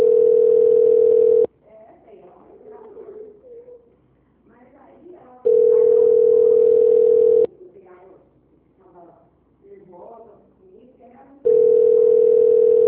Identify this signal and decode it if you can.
Ringback tone (North America), 440 + 480 Hz, with speech.
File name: out- -2006- - - .261